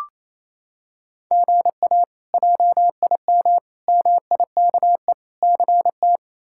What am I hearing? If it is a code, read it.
Morse code, heard as a keyed tone, about 27 wpm, 700 Hz. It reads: GA JIM MIKE CT
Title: K1USN SST QSO Practice - 28wpm